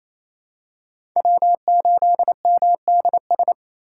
Morse code, heard as a keyed tone, about 28 wpm, 700 Hz. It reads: W8MDS
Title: USA Call Signs - 5 Characters - 2x Character Spacing - 28wpm